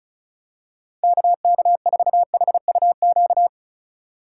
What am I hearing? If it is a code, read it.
Morse code, heard as a keyed tone, about 36 wpm, 700 Hz. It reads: KK4HUQ
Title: USA Call Signs - 6 Characters - 3x Character Spacing - 35wpm